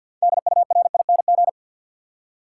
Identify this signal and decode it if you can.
Morse code, heard as a keyed tone, about 52 wpm, 700 Hz. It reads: DURING
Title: Six Letter Words - 50wpm